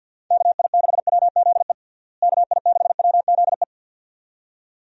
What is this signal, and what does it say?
Morse code, heard as a keyed tone, about 50 wpm, 700 Hz. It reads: KI6WZE KI6WZE
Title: USA Call Signs - 6 Characters - Repeated 2x - 50wpm